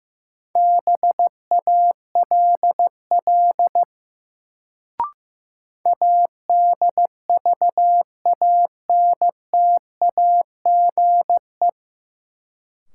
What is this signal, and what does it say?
Morse code, heard as a keyed tone, about 15 wpm, 700 Hz. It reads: BALL ADVANTAGE
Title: Sentences from Top 900 Words - New Word Drill - 15wpm